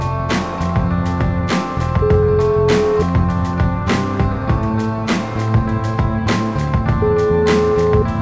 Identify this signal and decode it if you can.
Ringback tone (Europe, 425 Hz), with noise.